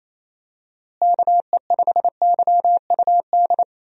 Morse code, heard as a keyed tone, about 28 wpm, 700 Hz. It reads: KE5YUD
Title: USA Call Signs - 6 Characters - Repeated 2x - 28wpm